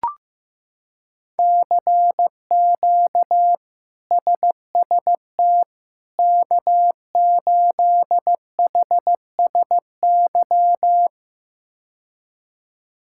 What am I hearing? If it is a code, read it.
Morse code, heard as a keyed tone, about 15 wpm, 700 Hz. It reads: CQ SST K8HSY
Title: K1USN SST QSO Practice - 15wpm